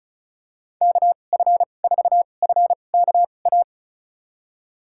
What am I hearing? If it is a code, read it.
Morse code, heard as a keyed tone, about 36 wpm, spaced as for 25 wpm, 700 Hz. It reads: KF4FKA
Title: USA Call Signs - 2x Character Spacing - 35wpm